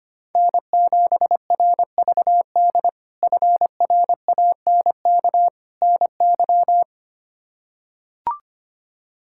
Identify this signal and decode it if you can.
Morse code, heard as a keyed tone, about 25 wpm, 700 Hz. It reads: N7RVD FRANK NY
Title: K1USN SST QSO Practice - 25wpm